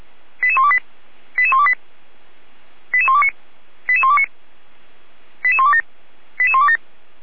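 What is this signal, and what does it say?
Five-tone selective call (ZVEI), heard as quick runs of five tones, tones of about 70 ms. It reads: 88128 88128 88129 88129 88127 88127
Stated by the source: ZVEI five-tone selective call received by SDR, from the feuerwehr-orlamuende.org (Feuerwehr Orlamünde) fire brigade archive.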